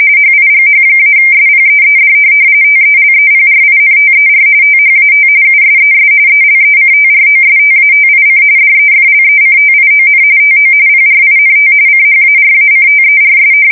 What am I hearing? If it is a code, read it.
RTTY, about 45.45 baud. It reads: HIGH-SPEED TELEPRINTER THAT USES ELECTROSTATIC DEFLECTION TO PRINT CHARACTERS
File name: fsk170 / teletype-corp